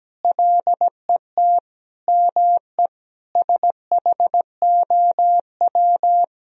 Morse code, heard as a keyed tone, about 17 wpm, 700 Hz. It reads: LET ME SHOW
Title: Sentences from Top 1000 Words - Review Part 2 - 17wpm